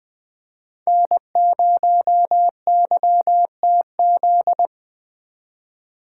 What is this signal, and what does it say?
Morse code, heard as a keyed tone, about 20 wpm, 700 Hz. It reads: N0YTZ